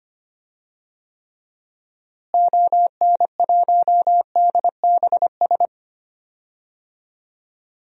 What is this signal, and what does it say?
Morse code, heard as a keyed tone, about 25 wpm, 700 Hz. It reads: ON1DBS
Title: DXCC-100 Call Signs - 6 Characters - Repeated 2x - 25wpm